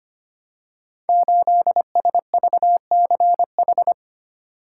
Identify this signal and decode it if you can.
Morse code, heard as a keyed tone, about 25 wpm, 700 Hz. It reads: 8SVCH
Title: Code Groups - 5 Letter:Numbers - 25wpm